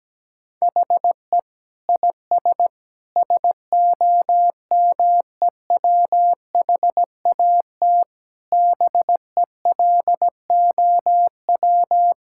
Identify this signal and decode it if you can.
Morse code, heard as a keyed tone, about 17 wpm, 700 Hz. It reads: HE IS SOMEWHAT BELOW
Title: Sentences from Top 1600 Words 17wpm